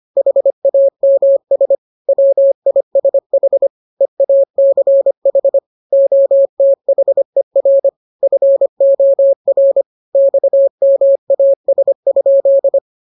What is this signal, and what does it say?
Morse code, heard as a keyed tone, about 25 wpm, 550 Hz. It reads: HAMS WISH EACH OTHER FOR XMAS?
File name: jokes / joke 17 25 0000